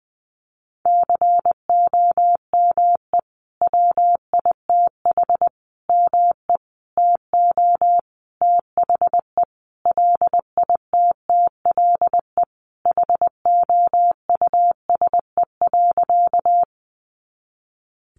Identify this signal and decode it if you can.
Morse code, heard as a keyed tone, about 20 wpm, 700 Hz. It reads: COME WITH ME TO THE LITTLE HOUSE.
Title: Sentences from Top 200 Words 20wpm